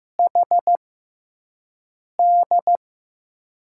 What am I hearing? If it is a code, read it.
Morse code, heard as a keyed tone, about 15 wpm, spaced as for 5 wpm, 700 Hz. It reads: HD
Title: USA Call Signs - 5 Characters - 6x Character Spacing - 15wpm